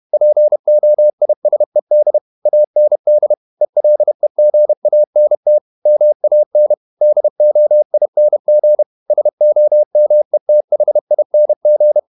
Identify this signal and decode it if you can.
Morse code, heard as a keyed tone, about 31 wpm, 600 Hz. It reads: POISED AND ELEGANT MAN DOING SOMETHING
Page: Never 31wpm | CW med Gnister